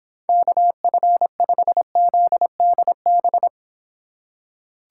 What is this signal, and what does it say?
Morse code, heard as a keyed tone, about 25 wpm, 700 Hz. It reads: KF5ZDB